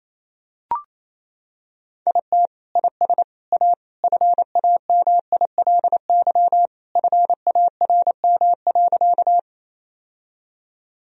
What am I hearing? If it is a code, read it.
Morse code, heard as a keyed tone, about 27 wpm, 700 Hz. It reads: IT IS A FAMILY FARM.
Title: Sentences from Top 1000 Words - Review Part 2 - 28wpm